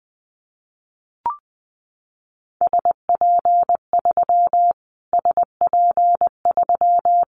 Morse code, heard as a keyed tone, about 20 wpm, 700 Hz. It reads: SP3 SP3
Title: Callsign Prefixes for Top 25 DXCC - Repeated 3x - 20wpm